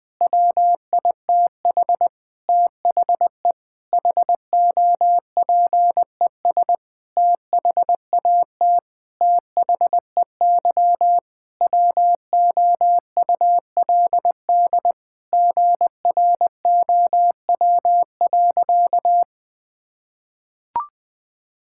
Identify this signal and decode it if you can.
Morse code, heard as a keyed tone, about 20 wpm, 700 Hz. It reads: WITH THE HOPES THAT THEY WOULD GROW.
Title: Sentences with 9-Word Phrases - Spoken Beginning - 20wpm